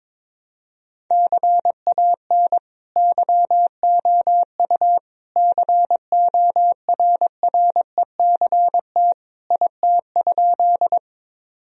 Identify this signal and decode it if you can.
Morse code, heard as a keyed tone, about 22 wpm, 700 Hz. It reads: CAN YOU CORRECT IT?
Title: Sentences from Top 1000 Words - Review Part 2 - 22wpm